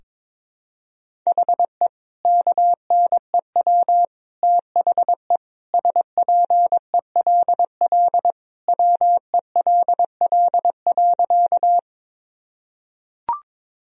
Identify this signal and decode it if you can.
Morse code, heard as a keyed tone, about 22 wpm, 700 Hz. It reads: HE KNEW THE SPELL WELL.
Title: Sentences from Top 1000 Words - Review Part 1 - 22wpm